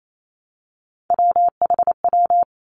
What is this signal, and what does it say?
Morse code, heard as a keyed tone, about 28 wpm, 700 Hz. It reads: WHW
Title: Three Challenging Letter-Number Combinations 28wpm